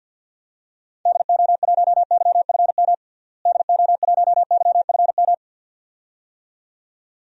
Morse code, heard as a keyed tone, about 52 wpm, 700 Hz. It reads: DO1YFM DO1YFM
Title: DXCC-100 Call Signs - 6 Characters - Repeated 2x - 50wpm